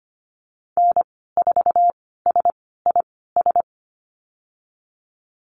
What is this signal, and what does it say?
Morse code, heard as a keyed tone, about 25 wpm, spaced as for 16 wpm, 690 Hz. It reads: N4SIS
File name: Lesson 15g - Number 5 - Call Signs 25wpm